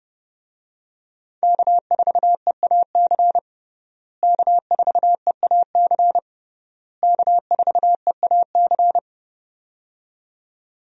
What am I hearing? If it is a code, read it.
Morse code, heard as a keyed tone, about 31 wpm, 700 Hz. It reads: K4EAC K4EAC K4EAC
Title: USA Call Signs - 5 Characters - Repeated 3x - 30wpm